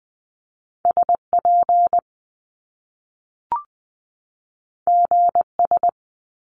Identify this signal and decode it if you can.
Morse code, heard as a keyed tone, about 20 wpm, 700 Hz. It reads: SP GS
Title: Top Two Letter Combinations in Words 20wpm